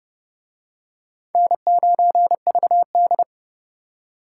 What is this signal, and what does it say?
Morse code, heard as a keyed tone, about 28 wpm, 700 Hz. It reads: N9VD